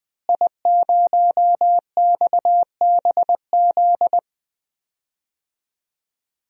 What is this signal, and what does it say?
Morse code, heard as a keyed tone, about 20 wpm, 700 Hz. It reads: I0XBZ
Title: DXCC-100 Call Signs - 5 Characters - Repeated 2x - 20wpm